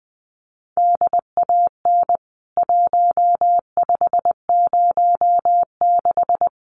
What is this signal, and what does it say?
Morse code, heard as a keyed tone, about 20 wpm, 700 Hz. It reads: DAN 1506
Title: CWT Exchange 20wpm